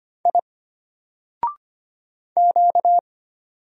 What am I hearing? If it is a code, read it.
Morse code, heard as a keyed tone, about 25 wpm, 700 Hz. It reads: I Q
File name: Single-Letter-Number-25wpm